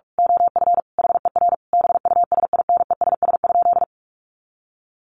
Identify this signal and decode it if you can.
Morse code, heard as a keyed tone, about 44 wpm, 700 Hz. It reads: OF HER BUSINESS?
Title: Sentences from Top 1100 Words 45wpm